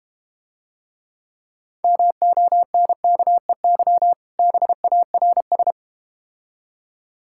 Morse code, heard as a keyed tone, about 30 wpm, 700 Hz. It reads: MONKEY BARS
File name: Easy Words - Speed-Racing with Hints - 25wpm